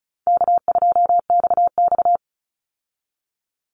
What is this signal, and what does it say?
Morse code, heard as a keyed tone, about 34 wpm, 700 Hz. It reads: K2XX
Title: Call Signs Repeated 3x - 35wpm